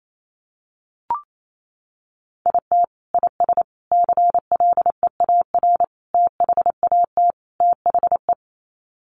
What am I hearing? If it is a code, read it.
Morse code, heard as a keyed tone, about 28 wpm, 700 Hz. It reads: IT IS CLEAR THAT THE
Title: Sets of 5 Words Encore 28wpm